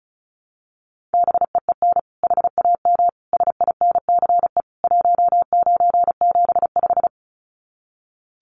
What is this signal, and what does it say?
Morse code, heard as a keyed tone, about 34 wpm, 700 Hz. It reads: BEEN HAM SINCE 1975
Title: Common QSO Phrases 35wpm